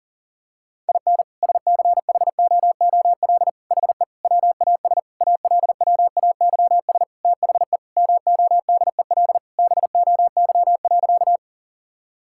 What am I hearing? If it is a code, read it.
Morse code, heard as a keyed tone, about 40 wpm, 700 Hz. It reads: IN SCHOOL HE WAS ALWAYS THE MODEL BOY.
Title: Sentences from Top 1000 Words - Review Part 5 - 40wpm